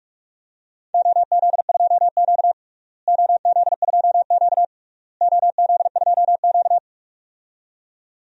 Morse code, heard as a keyed tone, about 46 wpm, 700 Hz. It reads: OZ2Q OZ2Q OZ2Q
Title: DXCC-100 Call Signs - 4 Characters - Repeated 3x - 45wpm